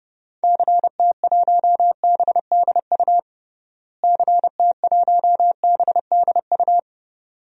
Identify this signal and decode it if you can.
Morse code, heard as a keyed tone, about 30 wpm, 700 Hz. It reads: CT1BDU CT1BDU
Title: DXCC-100 Call Signs - 6 Characters - Repeated 2x - 30wpm